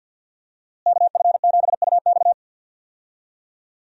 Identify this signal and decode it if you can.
Morse code, heard as a keyed tone, about 52 wpm, 700 Hz. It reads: KF7UX
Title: USA Call Signs - 5 Characters - Repeated 3x - 50wpm